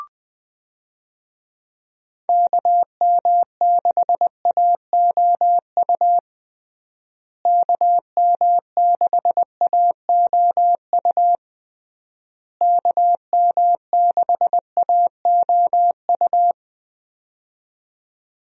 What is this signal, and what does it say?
Morse code, heard as a keyed tone, about 20 wpm, 700 Hz. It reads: KM6AOU KM6AOU KM6AOU
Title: USA Call Signs - 6 Characters - Repeated 3x - 20wpm